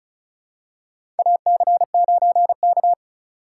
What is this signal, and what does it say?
Morse code, heard as a keyed tone, about 34 wpm, 700 Hz. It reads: AC9K
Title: USA Call Signs - 4 Characters - 4x Character Spacing - 35wpm